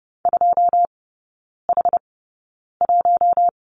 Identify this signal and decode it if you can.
Morse code, heard as a keyed tone, about 30 wpm, spaced as for 9 wpm, 700 Hz. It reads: 2H1
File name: 081 Lesson 14e - Number 2 - Three Characters 30wpm